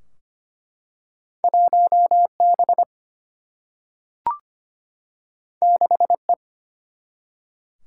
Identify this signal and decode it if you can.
Morse code, heard as a keyed tone, about 25 wpm, 700 Hz. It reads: 1 B 6 E
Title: Single-Letter Single-Number 25wpm